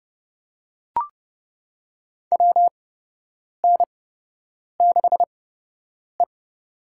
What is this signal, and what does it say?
Morse code, heard as a keyed tone, about 32 wpm, spaced as for 8 wpm, 700 Hz. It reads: WN6E
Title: USA Call Signs - 4 Characters - 8x Character Spacing - 30wpm